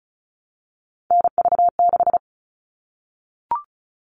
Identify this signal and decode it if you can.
Morse code, heard as a keyed tone, about 34 wpm, 700 Hz. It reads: NV6
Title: All Cut Numbers - 3-Digit 35wpm